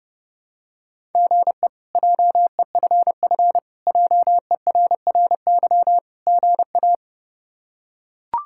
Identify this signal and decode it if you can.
Morse code, heard as a keyed tone, about 30 wpm, 700 Hz. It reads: GE JEFF JERRY GA
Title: K1USN SST QSO Practice - 30wpm